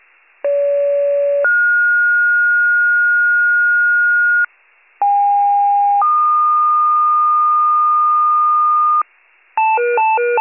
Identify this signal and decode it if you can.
Two-tone page, heard as a one-second tone, then a three-second tone, about 570 then 1430 Hz, about 810 then 1220 Hz, with noise. It is heard from 0.4 to 9.0 s.